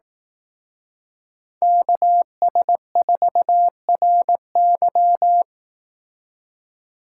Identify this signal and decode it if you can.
Morse code, heard as a keyed tone, about 18 wpm, 700 Hz. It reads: KS4RY
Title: Speed Racing - USA 2x2 Call Signs 15wpm